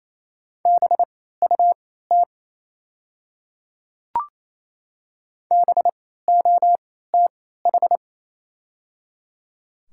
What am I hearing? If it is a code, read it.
Morse code, heard as a keyed tone, about 28 wpm, spaced as for 16 wpm, 700 Hz. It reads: BUT BOTH
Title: 3x Character Spacing - Top 300 Words - 28wpm